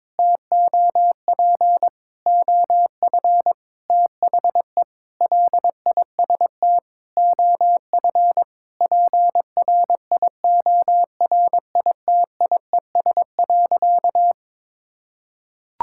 Morse code, heard as a keyed tone, about 22 wpm, 700 Hz. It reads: TOP OF THE LIST OF PRIORITIES.
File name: Sentences with 7-Word Phrases - 22wpm